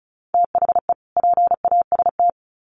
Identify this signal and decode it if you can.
Morse code, heard as a keyed tone, about 34 wpm, 700 Hz. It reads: THE PAST
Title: Sets of 2 Words 35wpm